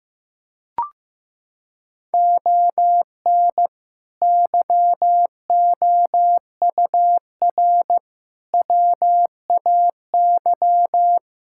Morse code, heard as a keyed tone, about 15 wpm, 700 Hz. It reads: ON YOUR WAY
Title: Sentences from Top 1000 Words - Review Part 6 - 15wpm